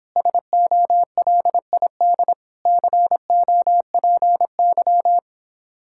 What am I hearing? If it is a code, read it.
Morse code, heard as a keyed tone, about 26 wpm, 700 Hz. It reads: SOLID COPY